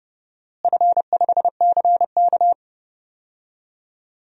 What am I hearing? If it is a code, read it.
Morse code, heard as a keyed tone, about 30 wpm, 700 Hz. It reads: F5CK
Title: DXCC-100 Call Signs - 4 Characters - Repeated 2x - 30wpm